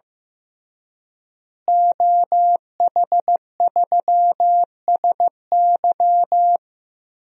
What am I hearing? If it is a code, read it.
Morse code, heard as a keyed tone, about 15 wpm, 700 Hz. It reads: OH3SY
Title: DXCC-100 Call Signs - 5 Characters - Repeated 2x - 15wpm